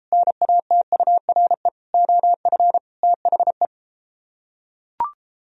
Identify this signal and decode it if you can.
Morse code, heard as a keyed tone, about 34 wpm, 700 Hz. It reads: NATURE OF THE